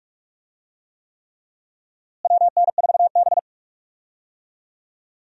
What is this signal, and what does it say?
Morse code, heard as a keyed tone, about 43 wpm, 700 Hz. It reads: WN4B
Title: USA Call Signs - 4 Characters - Repeated 3x - 45wpm